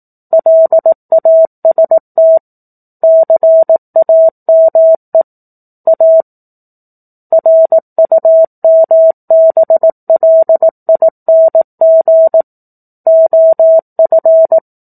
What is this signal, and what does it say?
Morse code, heard as a keyed tone, about 18 wpm, 650 Hz. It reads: LAST CAME A RUMBLING OF